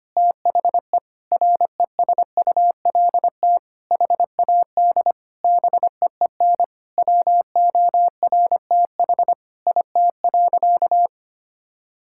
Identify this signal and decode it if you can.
Morse code, heard as a keyed tone, about 25 wpm, 700 Hz. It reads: THE RESULT HAD BEEN WORTH IT.
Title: Sentences from Top 1000 Words - Review Part 5 - 25wpm